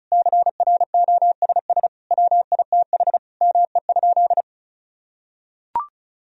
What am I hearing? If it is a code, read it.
Morse code, heard as a keyed tone, about 36 wpm, 700 Hz. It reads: CROSS WITH ME?
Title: Sentences from Top 1000 Words - Review Part 1 - 35wpm